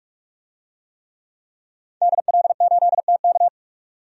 Morse code, heard as a keyed tone, about 46 wpm, 700 Hz. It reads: DL8TK
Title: DXCC-100 Call Signs - 5 Characters - 45wpm